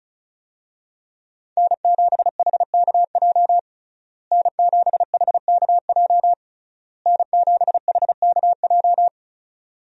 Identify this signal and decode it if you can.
Morse code, heard as a keyed tone, about 36 wpm, 700 Hz. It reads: N7HKJ N7HKJ N7HKJ
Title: USA Call Signs - 5 Characters - Repeated 3x - 35wpm